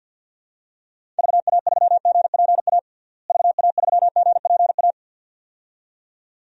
Morse code, heard as a keyed tone, about 52 wpm, 700 Hz. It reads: VA3GPA VA3GPA
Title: DXCC-100 Call Signs - 6 Characters - Repeated 2x - 50wpm